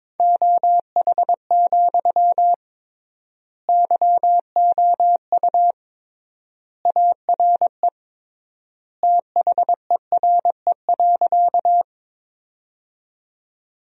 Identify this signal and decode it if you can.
Morse code, heard as a keyed tone, about 22 wpm, 700 Hz. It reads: OH, YOU ARE THERE.
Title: 3x Word-Spacing - Sentences from Top 500 Words - 22wpm